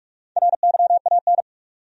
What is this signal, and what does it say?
Morse code, heard as a keyed tone, about 46 wpm, 700 Hz. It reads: RYAN
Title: USA Common Names - Max 5 Letters - 45wpm